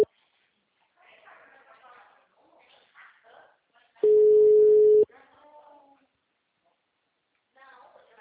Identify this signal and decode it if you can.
Ringback tone (Europe, 425 Hz), with speech.